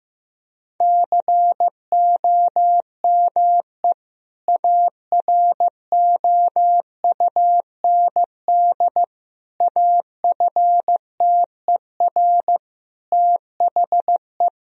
Morse code, heard as a keyed tone, about 15 wpm, 700 Hz. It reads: COME AROUND AFTER THE
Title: Sentences from Top 1000 Words - Review Part 3 - 15wpm